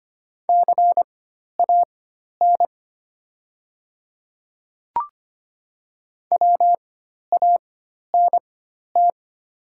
Morse code, heard as a keyed tone, about 25 wpm, spaced as for 12 wpm, 700 Hz. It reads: CAN WANT
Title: 4x Character Spacing - Top 300 Words - 25wpm